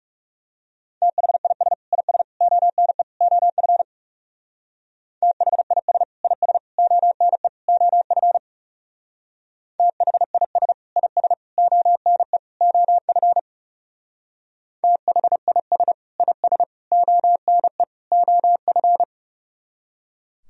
Morse code, heard as a keyed tone, about 36 wpm, 700 Hz. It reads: THIS IS ONE OF THIS IS ONE OF THIS IS ONE OF THIS IS ONE OF
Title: Speed-Racing - Sets of 4 Words 30wpm